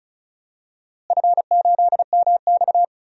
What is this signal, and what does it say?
Morse code, heard as a keyed tone, about 36 wpm, 700 Hz. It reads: F8MX
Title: DXCC-100 Call Signs - 4 Characters - Repeated 3x - 35wpm